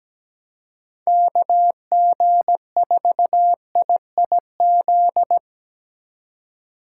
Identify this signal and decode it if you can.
Morse code, heard as a keyed tone, about 17 wpm, 700 Hz. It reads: KG4IIZ
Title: USA Call Signs - 6 Characters - 3x Character Spacing - 17wpm